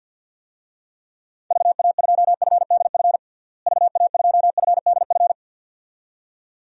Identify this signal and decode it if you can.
Morse code, heard as a keyed tone, about 47 wpm, 700 Hz. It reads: VA2FDF VA2FDF
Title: DXCC-100 Call Signs - 6 Characters - Repeated 2x - 50wpm